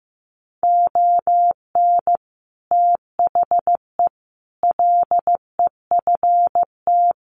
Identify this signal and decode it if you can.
Morse code, heard as a keyed tone, about 15 wpm, 700 Hz. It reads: ON THE LEFT
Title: Sentences from Top 700 Words 15wpm